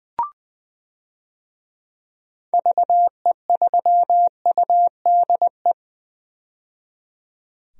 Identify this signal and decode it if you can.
Morse code, heard as a keyed tone, about 20 wpm, 700 Hz. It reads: VE3UDE
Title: DXCC-100 Call Signs - 6 Characters - 20wpm